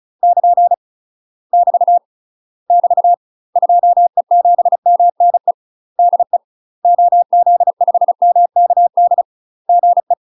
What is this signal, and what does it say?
Morse code, heard as a keyed tone, about 36 wpm, 700 Hz. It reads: <KN> <BT> <BT> 2E7MNE DE OZ5MKD GE
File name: UK MASTER QSOs TRAINING / 35WPM / UK MASTER QSOs 0031